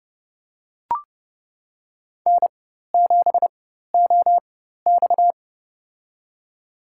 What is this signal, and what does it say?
Morse code, heard as a keyed tone, about 30 wpm, spaced as for 14 wpm, 700 Hz. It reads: N7OX
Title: USA Call Signs - 4x Character Spacing - 30wpm